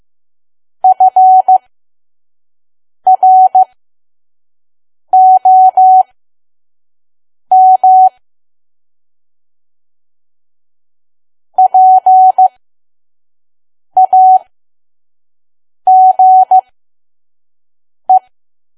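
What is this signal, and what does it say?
Morse code, heard as a keyed tone, about 15 wpm, spaced as for 5 wpm, 750 Hz. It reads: FROM PAGE
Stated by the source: Morse code practice at 5 WPM from KQ2P.COM.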